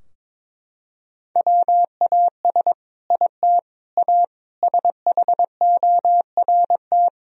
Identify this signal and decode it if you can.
Morse code, heard as a keyed tone, about 21 wpm, 700 Hz. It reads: WAS IT A SHORT
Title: Sentences from Top 1000 Words - Review Part 4 - 22wpm